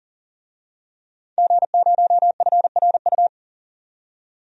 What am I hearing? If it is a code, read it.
Morse code, heard as a keyed tone, about 40 wpm, 700 Hz. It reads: G0FRU
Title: DXCC-100 Call Signs - 5 Characters - 40wpm